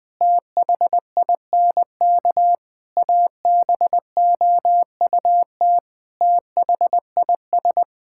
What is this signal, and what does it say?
Morse code, heard as a keyed tone, about 20 wpm, 700 Hz. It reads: THINK ABOUT THIS
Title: Sentences with 9-Word Phrases - 20wpm